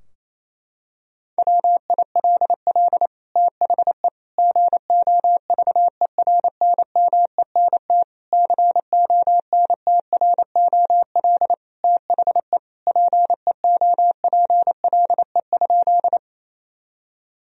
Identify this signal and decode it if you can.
Morse code, heard as a keyed tone, about 28 wpm, 700 Hz. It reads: WILL THE GOVERNMENT CONTROL THE PEOPLE?
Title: Sentences from Top 1000 Words - Review Part 3 - 28wpm